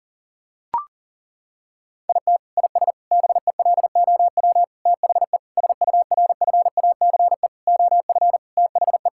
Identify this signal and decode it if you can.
Morse code, heard as a keyed tone, about 40 wpm, 700 Hz. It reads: IT IS BELOW THE SURFACE OF THE
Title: Sentences from Top 1000 Words - Review Part 5 - 40wpm